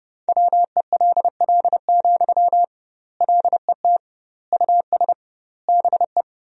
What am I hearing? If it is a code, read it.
Morse code, heard as a keyed tone, about 32 wpm, 700 Hz. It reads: WELL, LET US BE
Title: 2x Word-Spacing - Sentences from Top 300 Words - 30wpm